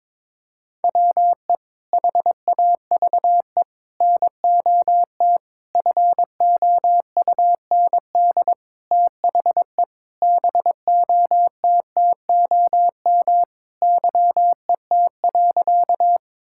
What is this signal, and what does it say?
Morse code, heard as a keyed tone, about 22 wpm, 700 Hz. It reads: WE HAVE NOT FOUND THE BOTTOM YET.
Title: Sentences from Top 1000 Words - Review Part 6 - 22wpm